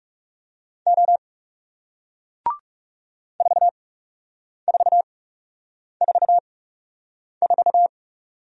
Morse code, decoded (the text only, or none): O 4444